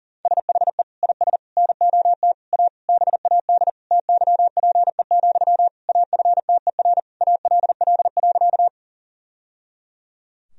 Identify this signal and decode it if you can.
Morse code, heard as a keyed tone, about 40 wpm, 700 Hz. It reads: SHE IS NOT A BAD TYPE, AFTER ALL.
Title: Sentences from Top 1000 Words - Review Part 3 - 40wpm